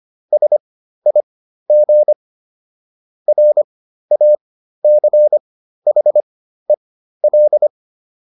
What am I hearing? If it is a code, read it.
Morse code, heard as a keyed tone, about 25 wpm, spaced as for 13 wpm, 600 Hz. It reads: SIG RACHEL
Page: intermediate-practice-files